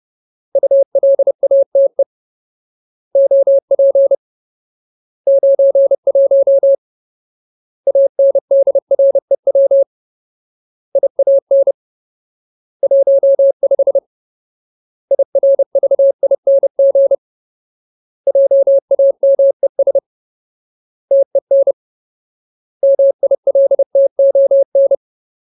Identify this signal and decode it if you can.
Morse code, heard as a keyed tone, about 30 wpm, 550 Hz. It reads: ULATE OP 91 ANDREW IAN 15 IRVING JAMES TEN MILTON
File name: Ultimate Word List / 30WPM Ultimate